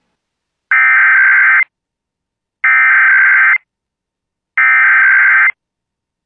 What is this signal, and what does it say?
Emergency Alert System (SAME) tones (header bursts), heard as harsh buzzing bursts. It is heard from 0.7 to 5.5 s.